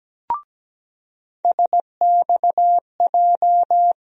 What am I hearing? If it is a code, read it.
Morse code, heard as a keyed tone, about 17 wpm, 700 Hz. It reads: SXJ
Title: USA Call Sign Postfixes - 3 Letters 17wpm